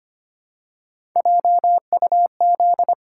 Morse code, heard as a keyed tone, about 25 wpm, 700 Hz. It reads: JUZ
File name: Three Letter-Number Combinations - 25wpm